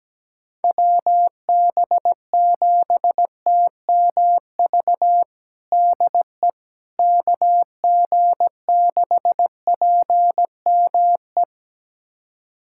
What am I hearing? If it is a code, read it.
Morse code, heard as a keyed tone, about 17 wpm, 700 Hz. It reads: WB7TMV DE KG6PME
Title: USA Callsign DE Callsign 17wpm